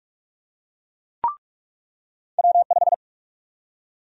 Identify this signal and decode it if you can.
Morse code, heard as a keyed tone, about 42 wpm, 700 Hz. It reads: W5
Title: USA Call Sign Prefixes - 45wpm